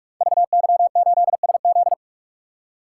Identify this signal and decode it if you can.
Morse code, heard as a keyed tone, about 44 wpm, 690 Hz. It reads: VY8SZ
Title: Code Groups - 5 Letter:Numbers - 45wpm